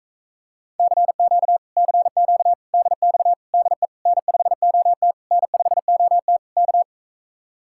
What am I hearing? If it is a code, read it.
Morse code, heard as a keyed tone, about 44 wpm, 700 Hz. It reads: CQ CQ DX DE N5OT N5OT K